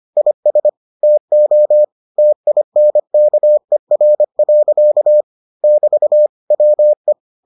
Morse code, heard as a keyed tone, about 25 wpm, 600 Hz. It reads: IS TO TINKER. <BT> WE